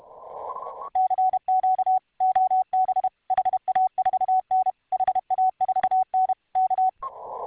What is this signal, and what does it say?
Morse code, heard as a keyed tone, about 31 wpm, 750 Hz. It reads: CQ OB HA4N HA4N K